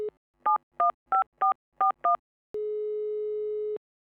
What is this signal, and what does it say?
DTMF, heard as telephone touch tones, tones of about 120 ms. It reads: 715441
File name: albums / SFX Technology Vol. 1 [288] / Swedish Telephone Dialling And Engaged [18062] - SFX Producer